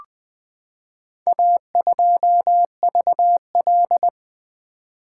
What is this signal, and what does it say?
Morse code, heard as a keyed tone, about 20 wpm, 700 Hz. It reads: A2VL
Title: Four Letter-Number Combinations - 20wpm